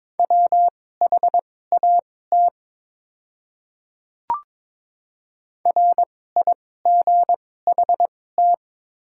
Morse code, heard as a keyed tone, about 22 wpm, spaced as for 16 wpm, 690 Hz. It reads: WHAT RIGHT
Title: 2x Character Spacing - Top 500 Words - 22wpm